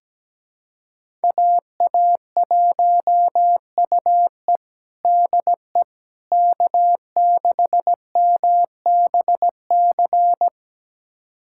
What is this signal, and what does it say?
Morse code, heard as a keyed tone, about 17 wpm, 700 Hz. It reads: AA1UE DE K6MBC
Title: USA Callsign DE Callsign - 5 Characters - 17wpm